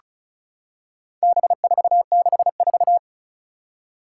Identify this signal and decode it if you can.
Morse code, heard as a keyed tone, about 35 wpm, 700 Hz. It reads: B464
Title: Four Challenging Letter-Number Combinations 35wpm